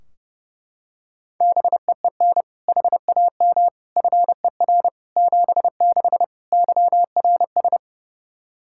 Morse code, heard as a keyed tone, about 30 wpm, 700 Hz. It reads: BEEN HAM FER 76 YRS
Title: Practice QSO - Exchange 3 - More Info - 30wpm